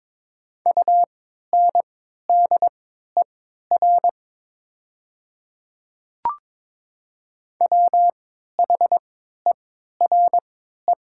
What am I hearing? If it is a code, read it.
Morse code, heard as a keyed tone, about 23 wpm, spaced as for 13 wpm, 700 Hz. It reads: UNDER WHERE
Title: 3x Character Spacing - Top 200 Words - 22wpm